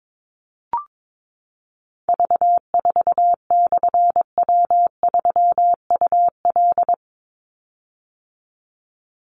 Morse code, heard as a keyed tone, about 22 wpm, 700 Hz. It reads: V4/W3UL
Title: Call Signs - 22wpm